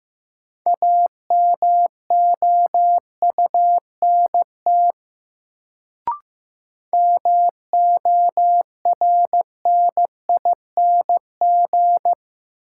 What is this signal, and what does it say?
Morse code, heard as a keyed tone, about 15 wpm, 700 Hz. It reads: AMOUNT MORNING
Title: Sets of 3 Words - New Word Drill - 15wpm